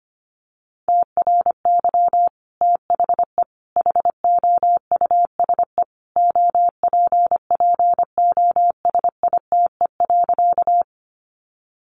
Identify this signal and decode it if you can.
Morse code, heard as a keyed tone, about 25 wpm, 700 Hz. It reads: TRY THE HOUSE OPPOSITE.
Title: Sentences from Top 1300 Words 25wpm